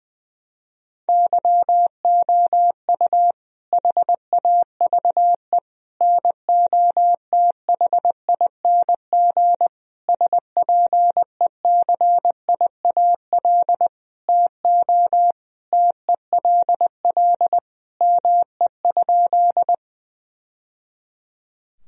Morse code, heard as a keyed tone, about 20 wpm, 700 Hz. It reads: YOU HAVE NOTHING SPECIAL TO TELL ME?
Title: Sentences from Top 1000 Words - Review Part 3 - 20wpm